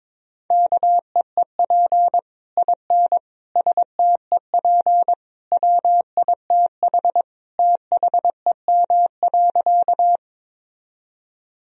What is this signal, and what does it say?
Morse code, heard as a keyed tone, about 22 wpm, 700 Hz. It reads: KEEP IN STEP WITH THEM.
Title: Sentences from Top 1000 Words - Review Part 2 - 22wpm